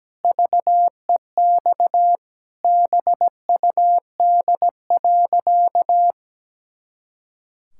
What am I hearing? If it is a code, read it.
Morse code, heard as a keyed tone, about 17 wpm, 700 Hz. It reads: VEX BUD.
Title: Pangram Challenge v2 - 17wpm